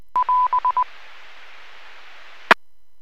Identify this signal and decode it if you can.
Morse code, heard as a keyed tone, about 20 wpm, 1000 Hz. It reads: <AS> E